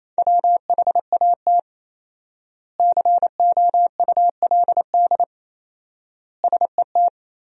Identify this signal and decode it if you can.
Morse code, heard as a keyed tone, about 28 wpm, 700 Hz. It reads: WHAT COULD SET
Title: 4x Word-Spacing - Sentences from Top 200 Words - 28wpm